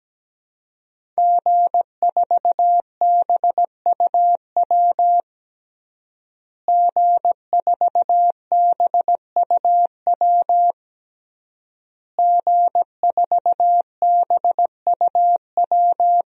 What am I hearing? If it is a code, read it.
Morse code, heard as a keyed tone, about 17 wpm, 700 Hz. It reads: G4BUW G4BUW G4BUW
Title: DXCC-100 Call Signs - 5 Characters - Repeated 3x - 17wpm